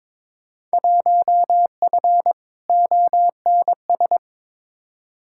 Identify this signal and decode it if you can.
Morse code, heard as a keyed tone, about 22 wpm, 700 Hz. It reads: 1F ONS
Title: Field Day Exchanges Encore - 22wpm